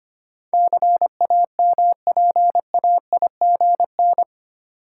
Morse code, heard as a keyed tone, about 25 wpm, 700 Hz. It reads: CAMPAIGN
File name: Eight Letter Words - 25wpm